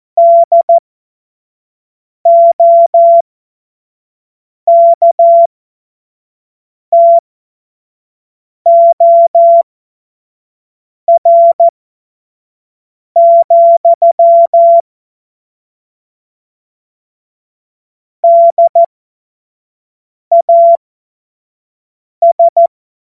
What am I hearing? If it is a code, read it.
Morse code, heard as a keyed tone, about 14 wpm, spaced as for 5 wpm, 680 Hz. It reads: DOKTOR, DAS
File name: Morsekurs / Witze / Witz 16 April 2024 25 F70